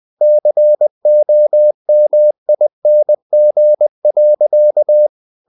Morse code, heard as a keyed tone, about 20 wpm, 600 Hz. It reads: COMING.
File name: the old man and the sea part 2 20wpm